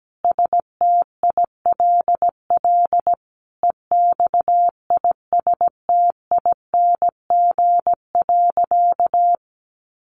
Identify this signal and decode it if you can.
Morse code, heard as a keyed tone, about 17 wpm, 700 Hz. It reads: STILL EXISTING.
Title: Sentences from Top 1500 Words 17wpm